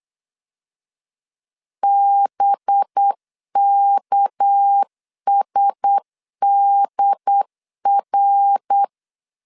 Morse code, heard as a keyed tone, about 8 wpm, 800 Hz. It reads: BKSDR
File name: LEZIONI / 6 PKX0 / 6 PK 8wpm 3003